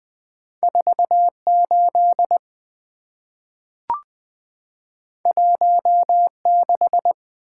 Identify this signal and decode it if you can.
Morse code, heard as a keyed tone, about 20 wpm, 690 Hz. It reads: 48 16
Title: Two Letter-Number Combinations - 20wpm